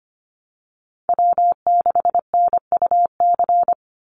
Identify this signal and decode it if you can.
Morse code, heard as a keyed tone, about 25 wpm, 700 Hz. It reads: W6NUC
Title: Call Signs Repeated 2x - 25wpm